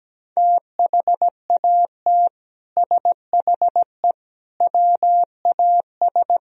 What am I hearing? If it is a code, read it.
Morse code, heard as a keyed tone, about 17 wpm, 700 Hz. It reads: THAT SHE WAS
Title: Speed-Racing - Sets of 3 Words 17wpm